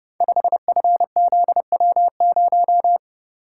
Morse code, heard as a keyed tone, about 30 wpm, 700 Hz. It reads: 5FZW0
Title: Code Groups - 5 Letter:Numbers - 30wpm